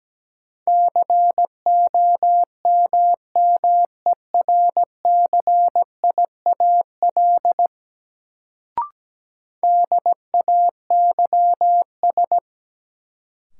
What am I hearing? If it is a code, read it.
Morse code, heard as a keyed tone, about 17 wpm, 700 Hz. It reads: COMMERCIAL DAYS
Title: Sentences from Top 1000 Words - Review Part 5 - New Word Drill - 17wpm